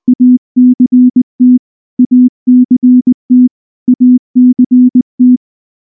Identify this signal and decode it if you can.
Morse code, heard as a keyed tone, about 19 wpm, 250 Hz. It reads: ACT ACT ACT
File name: morse / morse-training-20WPM-9